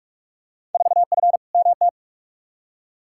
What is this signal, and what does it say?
Morse code, heard as a keyed tone, about 45 wpm, 700 Hz. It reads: 4F MT
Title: Field Day Exchanges Encore - 45wpm